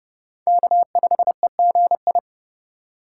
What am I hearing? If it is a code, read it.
Morse code, heard as a keyed tone, about 32 wpm, 700 Hz. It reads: K5EGI
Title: USA Call Signs - 5 Characters - 2x Character Spacing - 30wpm